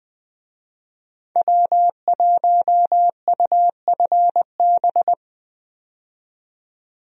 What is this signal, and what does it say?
Morse code, heard as a keyed tone, about 21 wpm, 700 Hz. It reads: W1UFB
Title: USA Call Signs - 5 Characters - 20wpm